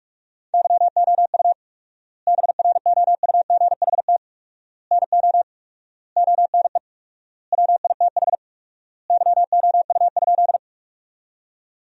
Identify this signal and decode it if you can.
Morse code, heard as a keyed tone, about 46 wpm, 700 Hz. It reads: YOU BROUGHT NO ONE WITH YOU?
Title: 4x Word-Spacing - Sentences from Top 500 Words - 45wpm